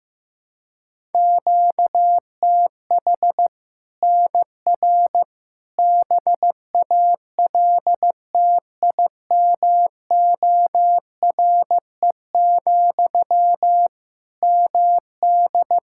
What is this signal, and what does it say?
Morse code, heard as a keyed tone, about 15 wpm, 700 Hz. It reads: QTH NR BALTIMORE, MD
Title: Practice QSO - Exchange 1 - Beginning - 15wpm